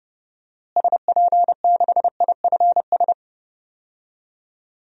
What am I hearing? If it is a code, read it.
Morse code, heard as a keyed tone, about 30 wpm, 700 Hz. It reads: SP6IFS
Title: DXCC-100 Call Signs - 6 Characters - Repeated 2x - 30wpm